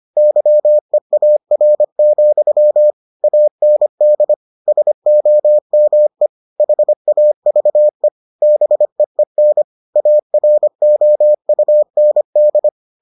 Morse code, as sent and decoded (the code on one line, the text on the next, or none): -.-- . .- .-. --..-- / .- -. -.. / ... --- -- . / .... .- ...- . / -... . -. / .- .-. --- ..- -. -..
YEAR, AND SOME HAVE BEEN AROUND